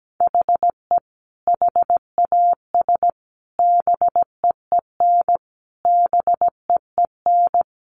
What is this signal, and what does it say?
Morse code, heard as a keyed tone, about 17 wpm, 700 Hz. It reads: HE HAS BEEN BEEN
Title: Sentences from Top 300 Words 17wpm